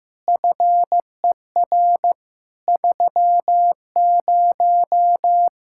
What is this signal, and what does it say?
Morse code, heard as a keyed tone, about 15 wpm, 700 Hz. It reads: FER 30
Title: Practice QSO - Exchange 3 - More Info - 15wpm